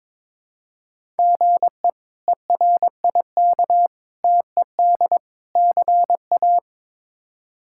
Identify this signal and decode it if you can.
Morse code, heard as a keyed tone, about 22 wpm, 700 Hz. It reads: GE ERIK TED CA
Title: K1USN SST QSO Practice - 22wpm